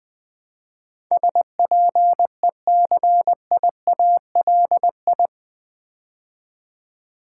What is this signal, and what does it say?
Morse code, heard as a keyed tone, about 20 wpm, 700 Hz. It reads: SPECIALI
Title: Build A Word - 4 Letters At A Time - 10-Letter Words - 20wpm